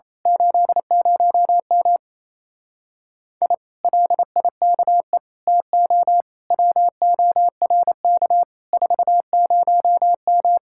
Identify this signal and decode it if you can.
Morse code, heard as a keyed tone, about 27 wpm, 700 Hz. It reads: 80M I LIKE TO WORK 40M
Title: Speed-Racing Common QSO Phrases Encore - 28wpm